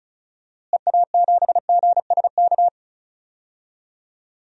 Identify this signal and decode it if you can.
Morse code, heard as a keyed tone, about 34 wpm, 700 Hz. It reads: EA7GSK
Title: DXCC-100 Call Signs - 6 Characters - Repeated 3x - 35wpm